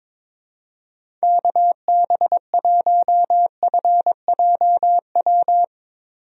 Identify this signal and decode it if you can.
Morse code, heard as a keyed tone, about 22 wpm, 700 Hz. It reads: KB1FJW
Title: USA Call Signs - 6 Characters - 22wpm